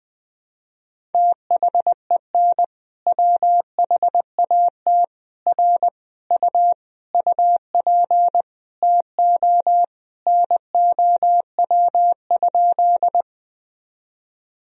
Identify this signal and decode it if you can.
Morse code, heard as a keyed tone, about 20 wpm, 700 Hz. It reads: THEN WHAT R U UP TO NOW?
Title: Sentences from Top 100 Words and Abbreviations 20wpm